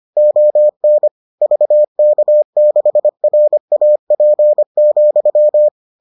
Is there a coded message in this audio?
Yes: Morse code from 0.2 to 5.7 s, heard as a keyed tone, about 25 wpm, 600 Hz. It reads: ON VK6RAP,